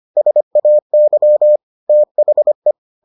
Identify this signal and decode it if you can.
Morse code, heard as a keyed tone, about 25 wpm, 600 Hz. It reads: SAY THE